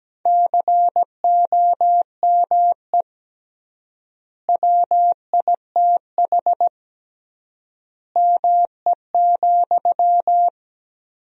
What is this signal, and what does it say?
Morse code, heard as a keyed tone, about 17 wpm, 700 Hz. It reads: COME WITH ME,
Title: 3x Word-Spacing - Sentences from Top 200 Words - 17wpm